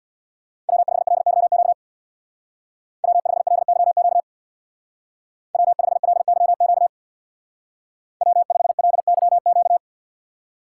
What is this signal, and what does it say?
Morse code, heard as a keyed tone, about 64 wpm, 700 Hz. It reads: W5LYQ W5LYQ W5LYQ W5LYQ